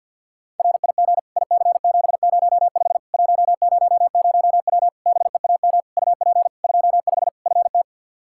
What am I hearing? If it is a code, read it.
Morse code, heard as a keyed tone, about 52 wpm, 700 Hz. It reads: RIG IC705 100W BEAM UP 25 FT
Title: K1USN SST QSO Practice 50wpm